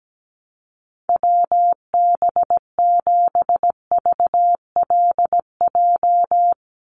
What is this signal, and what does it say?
Morse code, heard as a keyed tone, about 17 wpm, 700 Hz. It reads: WB7VLJ
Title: USA Call Signs 17wpm